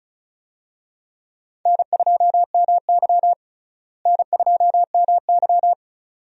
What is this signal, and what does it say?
Morse code, heard as a keyed tone, about 34 wpm, 700 Hz. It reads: N2MY N2MY
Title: USA Call Signs - 4 Characters - Repeated 2x - 35wpm